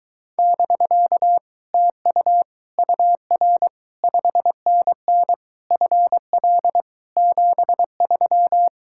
Morse code, heard as a keyed tone, about 24 wpm, 700 Hz. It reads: <BK> TU UR 5NN FL 73